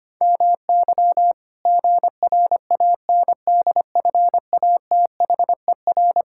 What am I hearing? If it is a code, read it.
Morse code, heard as a keyed tone, about 25 wpm, 700 Hz. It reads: MY GRANDFATHER
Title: Pangram Challenge v2 - 25wpm